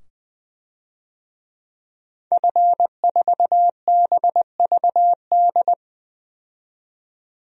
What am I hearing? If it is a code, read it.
Morse code, heard as a keyed tone, about 21 wpm, 700 Hz. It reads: F4BVD
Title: DXCC-100 Call Signs - 5 Characters - 20wpm